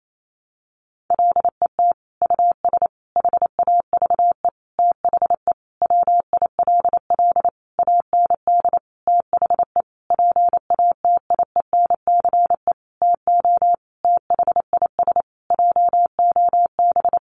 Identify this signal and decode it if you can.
Morse code, heard as a keyed tone, about 28 wpm, 700 Hz. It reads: LET US HAVE THE WILL AND THE PATIENCE TO THIS JOB
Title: Sentences from Top 1300 Words 28wpm